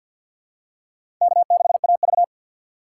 Morse code, heard as a keyed tone, about 51 wpm, 690 Hz. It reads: K6AV
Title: USA Call Signs - 2x Character Spacing - 50wpm